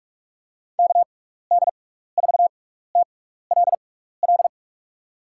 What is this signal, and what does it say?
Morse code, heard as a keyed tone, about 46 wpm, spaced as for 15 wpm, 700 Hz. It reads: KD4TLL
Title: USA Call Signs - 6 Characters - 6x Character Spacing - 45wpm